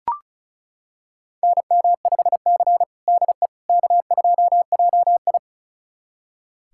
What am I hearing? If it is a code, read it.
Morse code, heard as a keyed tone, about 36 wpm, 690 Hz. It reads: NM5C DE K2JI
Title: USA Callsign DE Callsign - 4 Characters - 35wpm